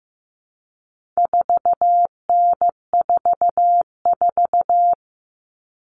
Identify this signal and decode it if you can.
Morse code, heard as a keyed tone, about 15 wpm, 700 Hz. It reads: 4N44
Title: All Cut Numbers - 4-Digit 15wpm